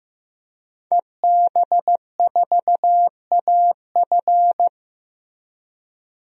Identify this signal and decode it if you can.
Morse code, heard as a keyed tone, about 15 wpm, 700 Hz. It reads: EB4AF
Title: DXCC-100 Call Signs - 5 Characters - 15wpm